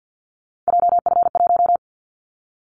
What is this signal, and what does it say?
Morse code, heard as a keyed tone, about 50 wpm, 700 Hz. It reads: JF1
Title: Callsign Prefixes for Top 25 DXCC - 50wpm